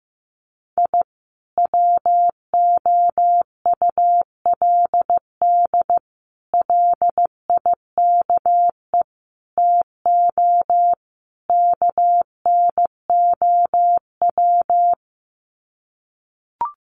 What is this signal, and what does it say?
Morse code, heard as a keyed tone, about 14 wpm, 700 Hz. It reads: I WOULD LIKE TO KNOW
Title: Sets of 5 Words Encore 15wpm